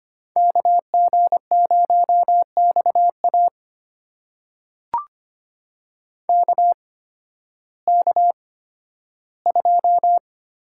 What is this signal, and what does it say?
Morse code, heard as a keyed tone, about 25 wpm, spaced as for 13 wpm, 700 Hz. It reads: KG0XA KK2
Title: USA Call Signs - 5 Characters - 8x Character Spacing - 25wpm